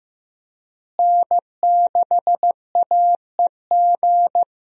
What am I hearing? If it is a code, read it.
Morse code, heard as a keyed tone, about 15 wpm, 700 Hz. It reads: N6AEG
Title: USA Call Signs - 5 Characters - Repeated 2x - 15wpm